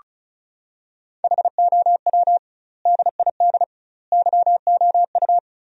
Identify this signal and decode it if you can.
Morse code, heard as a keyed tone, about 36 wpm, 700 Hz. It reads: HOW DID YOU
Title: 2x Word-Spacing - Sentences from Top 200 Words - 35wpm